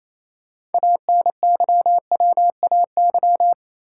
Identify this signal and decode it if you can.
Morse code, heard as a keyed tone, about 28 wpm, 700 Hz. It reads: ANYWAY
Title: Six Letter Words - 28wpm